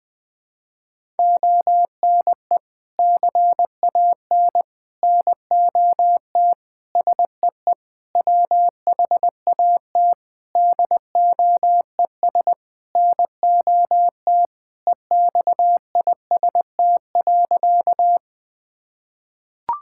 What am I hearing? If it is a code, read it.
Morse code, heard as a keyed tone, about 20 wpm, 700 Hz. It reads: ONE CAN NOT SEE WHAT DOES NOT EXIST.
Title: Sentences from Top 1000 Words - Review Part 5 - 20wpm